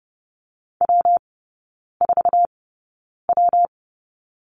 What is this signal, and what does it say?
Morse code, heard as a keyed tone, about 30 wpm, spaced as for 9 wpm, 700 Hz. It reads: W4W
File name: Lesson 19e - Letter w - Three Characters 30wpm